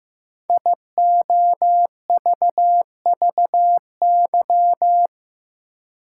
Morse code, heard as a keyed tone, about 15 wpm, 700 Hz. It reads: IOVVY
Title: Code Groups - 5 Letter:Numbers - 15wpm